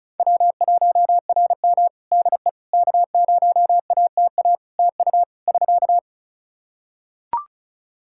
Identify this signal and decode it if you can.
Morse code, heard as a keyed tone, about 36 wpm, 700 Hz. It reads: W1RM DE K0ATA TU <SK>
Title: Practice QSO - Exchange 4 - Ending - 35wpm